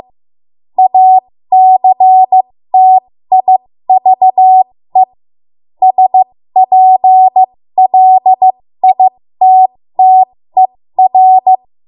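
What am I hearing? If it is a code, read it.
Morse code, heard as a keyed tone, about 15 wpm, spaced as for 13 wpm, 750 Hz. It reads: ACTIVE SPLITTER
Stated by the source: 13 wpm Morse code practice archive transmission.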